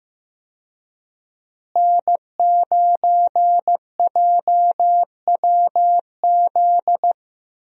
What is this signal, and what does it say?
Morse code, heard as a keyed tone, about 15 wpm, 700 Hz. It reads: N9JWZ
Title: USA Call Signs - 5 Characters - Repeated 2x - 15wpm